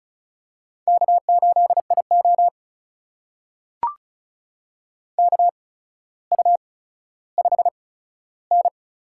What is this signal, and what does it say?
Morse code, heard as a keyed tone, about 36 wpm, spaced as for 15 wpm, 700 Hz. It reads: K8IO KU5N
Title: USA Call Signs - 4 Characters - 8x Character Spacing - 35wpm